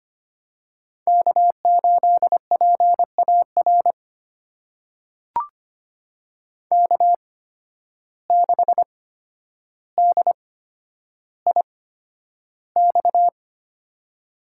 Morse code, heard as a keyed tone, about 25 wpm, spaced as for 11 wpm, 700 Hz. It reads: K8PAR K6DIX
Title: USA Call Signs - 5 Characters - 8x Character Spacing - 25wpm